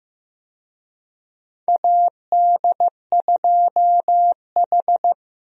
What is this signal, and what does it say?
Morse code, heard as a keyed tone, about 15 wpm, 700 Hz. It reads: AD2H
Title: USA Call Signs - 4 Characters - 15wpm